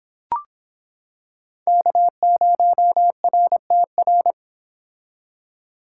Morse code, heard as a keyed tone, about 26 wpm, 700 Hz. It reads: K0RTR